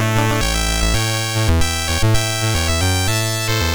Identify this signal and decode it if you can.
There is no code.